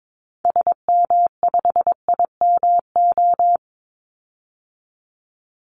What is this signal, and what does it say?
Morse code, heard as a keyed tone, about 22 wpm, 700 Hz. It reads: SM5IMO
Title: Call Signs Repeated 3x - 22wpm